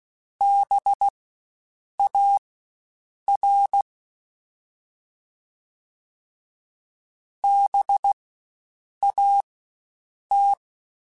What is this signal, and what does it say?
Morse code, heard as a keyed tone, about 16 wpm, spaced as for 7 wpm, 800 Hz. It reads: BAR BAT
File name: Spaced Alphas / Spaced Alphas mp3s / Bs sp @15